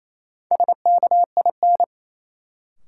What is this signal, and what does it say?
Morse code, heard as a keyed tone, about 28 wpm, 700 Hz. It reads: SKIN
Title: Sentences from Top 1300 Words - New Word Drill Alt - 28wpm